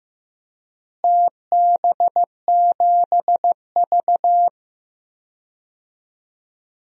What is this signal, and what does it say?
Morse code, heard as a keyed tone, about 15 wpm, 700 Hz. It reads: TB7V
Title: DXCC-100 Call Signs - 4 Characters - Repeated 3x - 15wpm